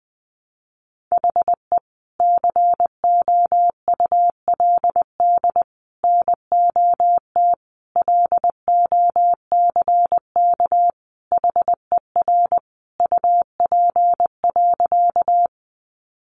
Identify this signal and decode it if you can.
Morse code, heard as a keyed tone, about 20 wpm, 700 Hz. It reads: HE COULD NOT LOCK HER UP.
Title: Sentences from Top 1300 Words 20wpm